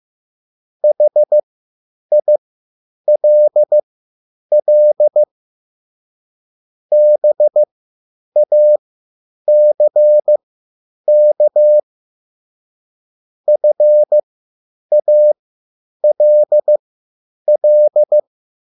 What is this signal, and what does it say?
Morse code, heard as a keyed tone, about 15 wpm, spaced as for 9 wpm, 600 Hz. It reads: HILL BACK FALL